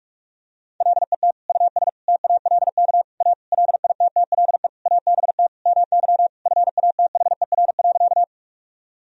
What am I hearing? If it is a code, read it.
Morse code, heard as a keyed tone, about 46 wpm, 700 Hz. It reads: LET US TALK A LITTLE ABT MY FATHER.
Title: Sentences from Top 400 Words and Abbreviations 45wpm